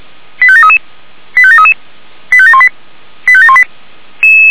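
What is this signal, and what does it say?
Five-tone selective call (ZVEI), heard as quick runs of five tones, tones of about 70 ms. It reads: 85733 85733 85718 85718
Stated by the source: ZVEI five-tone selective call received with an SDR, logged by Feuerwehr Orlamünde.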